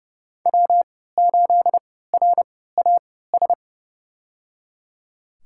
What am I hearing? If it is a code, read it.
Morse code, heard as a keyed tone, about 29 wpm, spaced as for 17 wpm, 700 Hz. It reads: W8RAS
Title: USA Call Signs - 5 Characters - 3x Character Spacing - 30wpm